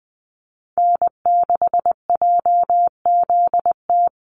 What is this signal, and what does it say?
Morse code, heard as a keyed tone, about 20 wpm, 700 Hz. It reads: N6JZT
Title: NA SOTA Activator Callsigns 20wpm